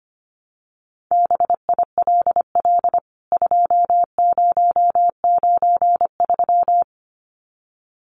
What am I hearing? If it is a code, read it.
Morse code, heard as a keyed tone, about 25 wpm, 700 Hz. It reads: BILL 2093
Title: CWT Exchange 25wpm